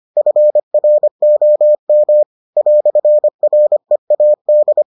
Morse code, heard as a keyed tone, about 25 wpm, 600 Hz. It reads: FROM "READ